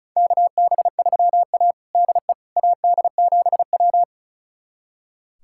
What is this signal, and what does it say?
Morse code, heard as a keyed tone, about 36 wpm, 700 Hz. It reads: KB3A DE AD7W
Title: USA Callsign DE Callsign - 4 Characters - 35wpm